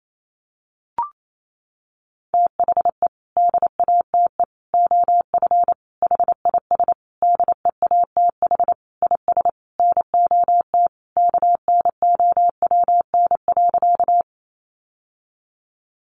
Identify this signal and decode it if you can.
Morse code, heard as a keyed tone, about 28 wpm, 700 Hz. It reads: THE DATE OF HIS DEATH IS NOT KNOWN.
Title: Sentences from Top 1100 Words 28wpm